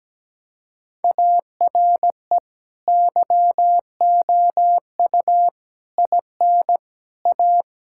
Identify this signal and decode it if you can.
Morse code, heard as a keyed tone, about 17 wpm, 700 Hz. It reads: ARE YOU IN A
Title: Sentences from Top 1000 Words - Review Part 6 - 17wpm